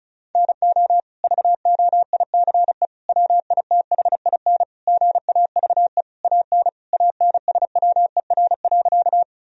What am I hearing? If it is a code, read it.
Morse code, heard as a keyed tone, about 34 wpm, 700 Hz. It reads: NO VOICE WITHIN GAVE AN ANSWER.
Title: Sentences from Top 1000 Words - Review Part 4 - 35wpm